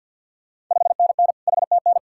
Morse code, heard as a keyed tone, about 47 wpm, 700 Hz. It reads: 5NN HTN
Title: Ontario QSO Party Exchange 50wpm